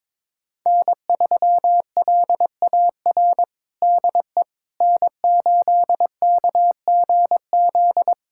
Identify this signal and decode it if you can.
Morse code, heard as a keyed tone, about 22 wpm, 700 Hz. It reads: N3LAR DE N8KGZ
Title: USA Callsign DE Callsign - 5 Characters - 22wpm